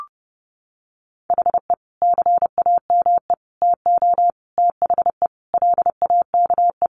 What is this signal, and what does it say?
Morse code, heard as a keyed tone, about 30 wpm, 700 Hz. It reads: HE CAME TO THE LAKE
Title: Sentences from Top 1100 Words 30wpm